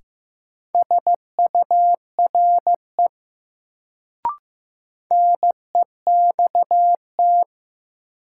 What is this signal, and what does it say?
Morse code, heard as a keyed tone, about 15 wpm, 700 Hz. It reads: SURE NEXT
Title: Sentences from Top 300 Words - New Word Drill - 15wpm